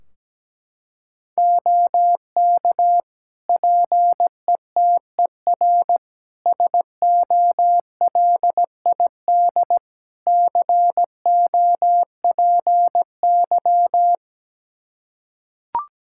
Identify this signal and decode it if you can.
Morse code, heard as a keyed tone, about 17 wpm, 700 Hz. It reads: OK PETER SOLID COPY
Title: K1USN SST QSO Practice 17wpm